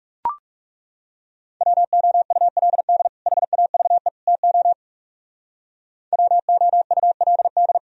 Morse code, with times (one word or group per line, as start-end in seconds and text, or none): 1.6-3.1 WOULD
3.3-4.1 HAVE
4.3-4.7 TO
6.1-7.8 WOULD